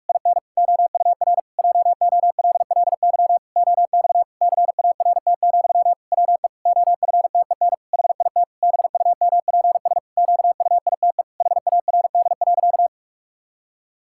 Morse code, heard as a keyed tone, about 46 wpm, 700 Hz. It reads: IN OUR JOLLY OX CART, WE OFTEN HIT BUMPS QUITE HARD.
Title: Pangram Challenge v2 - 45wpm